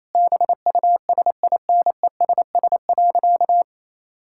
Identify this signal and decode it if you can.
Morse code, heard as a keyed tone, about 28 wpm, 700 Hz. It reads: BUSINESS.
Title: Sentences from Top 1000 Words - Review Part 5 - 28wpm